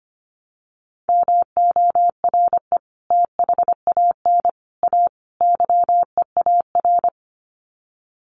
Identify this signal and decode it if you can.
Morse code, heard as a keyed tone, about 25 wpm, 700 Hz. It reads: MORE THAN A YEAR
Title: Sets of 4 Words 25wpm